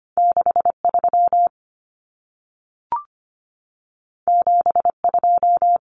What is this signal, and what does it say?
Morse code, heard as a keyed tone, about 25 wpm, 700 Hz. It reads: 63 72
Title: Common Cut Numbers - 2-Digit 25wpm